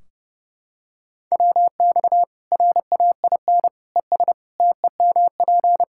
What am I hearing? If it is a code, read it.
Morse code, heard as a keyed tone, about 30 wpm, 700 Hz. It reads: WX RAIN ES TEMP
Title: Practice QSO - Exchange 2 - Info - 30wpm